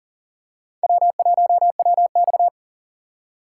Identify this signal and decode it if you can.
Morse code, heard as a keyed tone, about 40 wpm, 700 Hz. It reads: W1WX
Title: USA Call Signs - 4 Characters - 2x Character Spacing - 40wpm